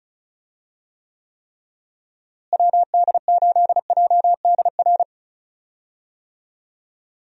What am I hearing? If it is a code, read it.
Morse code, heard as a keyed tone, about 36 wpm, 700 Hz. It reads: WD8JDR
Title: USA Call Signs - 6 Characters - Repeated 3x - 35wpm